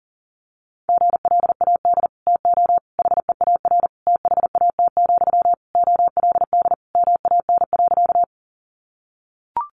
Morse code, heard as a keyed tone, about 37 wpm, 700 Hz. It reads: GLAD TO HEAR THAT, OLD MAN.
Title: Sentences from Top 800 Words 40wpm